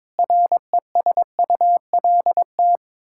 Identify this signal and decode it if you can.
Morse code, heard as a keyed tone, about 22 wpm, 700 Hz. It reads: RESULT
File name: practice / 22wpm / Build A Phrase - 5-Word Phrases - 22wpm